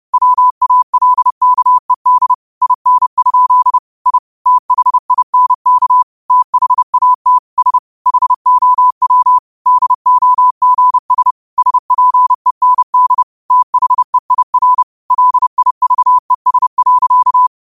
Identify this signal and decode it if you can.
Morse code, heard as a keyed tone, about 30 wpm, 1000 Hz. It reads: WALKED IN? I THINK THAT S HOW DOGS SPEND THEIR LIVES.
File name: qotdfiles / 30wpm